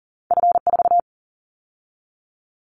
Morse code, heard as a keyed tone, about 40 wpm, 700 Hz. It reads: F4
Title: Callsign Prefixes for Top 50 DXCC - 40wpm